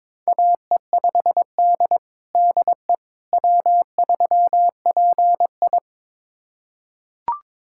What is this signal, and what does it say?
Morse code, heard as a keyed tone, about 22 wpm, 700 Hz. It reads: AE5D DE W3PI
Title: USA Callsign DE Callsign - 4 Characters - 22wpm